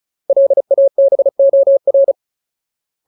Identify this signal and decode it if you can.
Morse code, heard as a keyed tone, about 34 wpm, 550 Hz. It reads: LABOR
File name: Ultimate Word List / 35WPM Ultimate